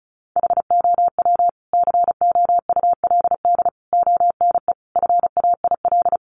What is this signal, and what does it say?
Morse code, heard as a keyed tone, about 34 wpm, 700 Hz. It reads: HOW COULD ONE FAIL
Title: Sentences from Top 1100 Words 35wpm